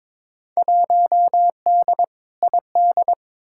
Morse code, heard as a keyed tone, about 22 wpm, 700 Hz. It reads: 1D ID
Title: Field Day Exchanges - 22wpm